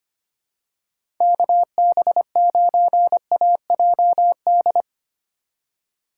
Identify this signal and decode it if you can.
Morse code, heard as a keyed tone, about 25 wpm, 700 Hz. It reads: KB9AJD
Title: USA Call Signs - 6 Characters - 4x Character Spacing - 25wpm